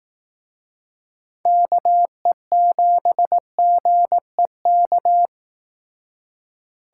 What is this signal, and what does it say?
Morse code, heard as a keyed tone, about 18 wpm, 700 Hz. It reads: KE7GEK
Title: Speed Racing - USA Call Signs 15wpm